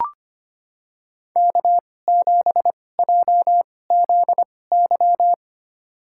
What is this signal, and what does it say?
Morse code, heard as a keyed tone, about 25 wpm, spaced as for 18 wpm, 700 Hz. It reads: K7JZY
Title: USA Call Signs - 5 Characters - 2x Character Spacing - 25wpm